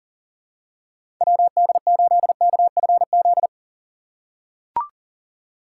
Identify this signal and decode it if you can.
Morse code, heard as a keyed tone, about 40 wpm, 700 Hz. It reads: WD8KFZ
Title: USA Call Signs - 3x Character Spacing - 40wpm